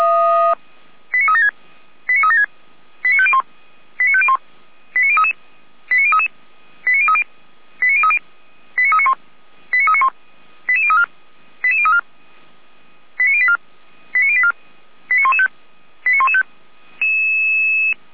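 Five-tone selective call (ZVEI), heard as quick runs of five tones, tones of about 70 ms. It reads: 89376 89376 89501 89501 89033 89033 89030 89030 89391 89391 88034 88034 89084 89084 89115 89115
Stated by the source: ZVEI five-tone selective call received by SDR.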